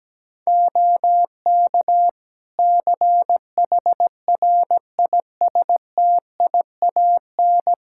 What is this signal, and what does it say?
Morse code, heard as a keyed tone, about 17 wpm, 700 Hz. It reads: OK CHRISTIAN
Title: Practice QSO - Exchange 3 - More Info - 17wpm